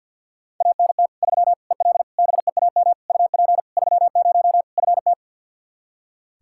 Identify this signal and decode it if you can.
Morse code, heard as a keyed tone, about 47 wpm, 700 Hz. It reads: ANT 3 EL BEAM UP 30 FT
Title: Practice QSO - Exchange 2 - Info - 50wpm